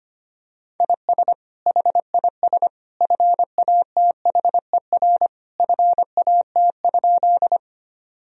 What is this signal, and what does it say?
Morse code, heard as a keyed tone, about 25 wpm, 700 Hz. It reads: IS HIS FATHER FAT?
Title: Sentences from Top 1000 Words - Review Part 4 - 25wpm